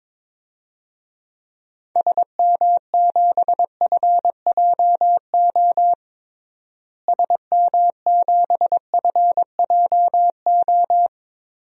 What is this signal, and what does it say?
Morse code, heard as a keyed tone, about 21 wpm, 700 Hz. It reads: SM7FJO SM7FJO
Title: DXCC-100 Call Signs - 6 Characters - Repeated 2x - 22wpm